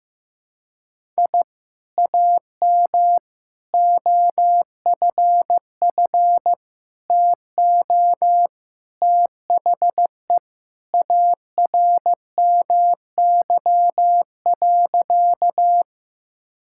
Morse code, heard as a keyed tone, about 14 wpm, 700 Hz. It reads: I AM OFF TO THE ARMY.
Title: Sentences from Top 1000 Words - Review Part 5 - 15wpm